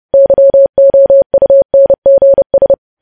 Morse code, heard as a keyed tone, about 30 wpm, 550 Hz. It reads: YOUNGS
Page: news headlines in morse code at 30 wpm